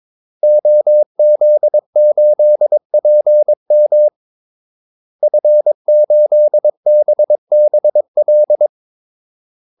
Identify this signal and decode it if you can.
Morse code, heard as a keyed tone, about 22 wpm, 600 Hz. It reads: OZ8PM F8BBL
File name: Callsigns-22wpm